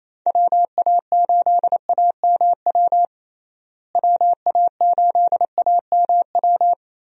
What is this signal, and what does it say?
Morse code, heard as a keyed tone, about 27 wpm, 700 Hz. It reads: WA8AMW WA8AMW
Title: USA Call Signs - 6 Characters - Repeated 3x - 28wpm